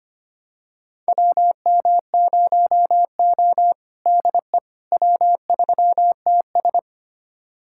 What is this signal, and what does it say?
Morse code, heard as a keyed tone, about 25 wpm, 700 Hz. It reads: WM0O DE W3TS
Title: USA Callsign DE Callsign - 4 Characters - 25wpm